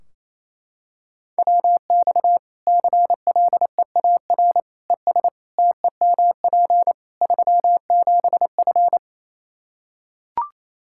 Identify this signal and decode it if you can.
Morse code, heard as a keyed tone, about 27 wpm, 700 Hz. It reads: WX CLEAR ES TEMP 37F
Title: K1USN SST QSO Practice 28wpm